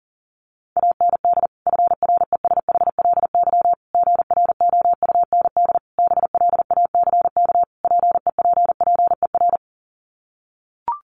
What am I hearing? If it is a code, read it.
Morse code, heard as a keyed tone, about 37 wpm, 700 Hz. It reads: AND FRESHLY GROUND BLACK PEPPER
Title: Sets of 5 Words Encore 40wpm